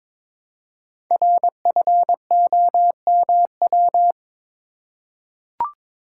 Morse code, heard as a keyed tone, about 22 wpm, 700 Hz. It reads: RFOMW
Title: Code Groups - 5 Letters - 22wpm